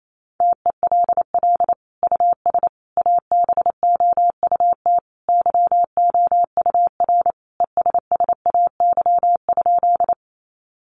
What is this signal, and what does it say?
Morse code, heard as a keyed tone, about 28 wpm, 700 Hz. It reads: TELL US ABOUT YOUR ESSAY?
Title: Sentences from Top 1500 Words 28wpm